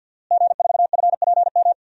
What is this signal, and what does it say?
Morse code, heard as a keyed tone, about 52 wpm, 700 Hz. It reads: G4FPM
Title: DXCC-100 Call Signs - 5 Characters - Repeated 2x - 50wpm